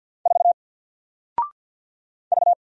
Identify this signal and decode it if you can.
Morse code, heard as a keyed tone, about 49 wpm, 700 Hz. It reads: 4 V
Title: Single Character (Letter:Number:Punctuation:Prosign) 50wpm